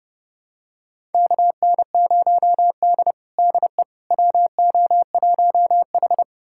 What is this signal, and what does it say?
Morse code, heard as a keyed tone, about 32 wpm, 700 Hz. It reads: KN0D DE WO1H
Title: USA Callsign DE Callsign - 4 Characters - 30wpm